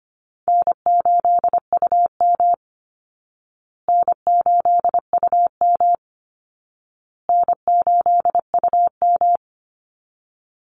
Morse code, heard as a keyed tone, about 25 wpm, 700 Hz. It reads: N8UM N8UM N8UM
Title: Easy Call Signs Repeated 3x - 25wpm